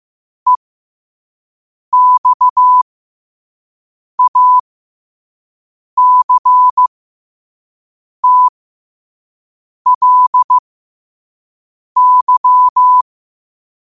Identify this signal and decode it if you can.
Morse code, heard as a keyed tone, about 15 wpm, spaced as for 5 wpm, 1000 Hz. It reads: EXACTLY
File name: 5wpm